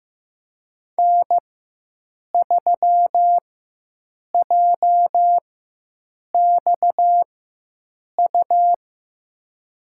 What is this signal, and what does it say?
Morse code, heard as a keyed tone, about 15 wpm, spaced as for 7 wpm, 700 Hz. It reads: N3JXU
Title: USA Call Signs - 5 Characters - 4x Character Spacing - 15wpm